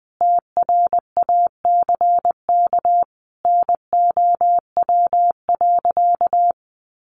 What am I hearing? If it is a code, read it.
Morse code, heard as a keyed tone, about 19 wpm, 700 Hz. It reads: TRACK NOW.
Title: Sentences from Top 1100 Words 20wpm